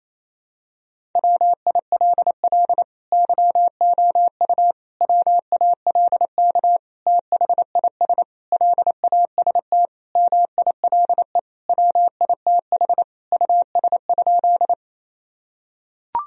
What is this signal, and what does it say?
Morse code, heard as a keyed tone, about 28 wpm, 700 Hz. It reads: WILL YOU WALK THIS LAST MILE WITH US?
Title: Sentences from Top 1000 Words - Review Part 1 - 28wpm